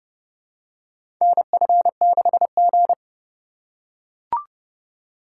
Morse code, heard as a keyed tone, about 29 wpm, 700 Hz. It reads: NF6G
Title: USA Call Signs - 4 Characters - 4x Character Spacing - 30wpm